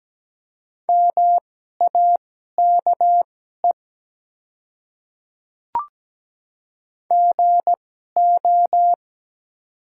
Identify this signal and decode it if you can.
Morse code, heard as a keyed tone, about 17 wpm, spaced as for 12 wpm, 690 Hz. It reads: MAKE GO
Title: 2x Character Spacing - Top 300 Words - 17wpm